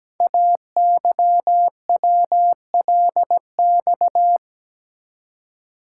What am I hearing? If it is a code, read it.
Morse code, heard as a keyed tone, about 17 wpm, 700 Hz. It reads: AYWLX
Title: Code Groups - 5 Letters - 17wpm